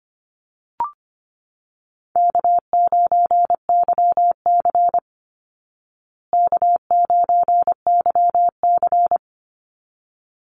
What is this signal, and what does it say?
Morse code, heard as a keyed tone, about 25 wpm, 700 Hz. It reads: K9YC K9YC
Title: Call Signs Repeated 3x - 25wpm